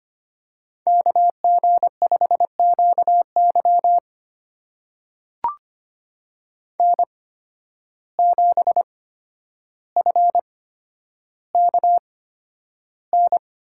Morse code, heard as a keyed tone, about 25 wpm, spaced as for 11 wpm, 700 Hz. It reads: KG5QY N7FKN
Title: USA Call Signs - 5 Characters - 8x Character Spacing - 25wpm